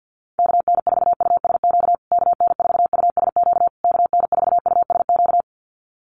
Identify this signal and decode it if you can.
Morse code, heard as a keyed tone, about 50 wpm, 700 Hz. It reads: KN4USQ KN4USQ KN4USQ
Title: USA Call Signs Repeated 3x 50wpm